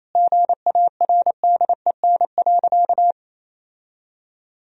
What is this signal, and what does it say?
Morse code, heard as a keyed tone, about 28 wpm, 700 Hz. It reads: GARDEN.
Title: Sentences from Top 1000 Words - Review Part 4 - 28wpm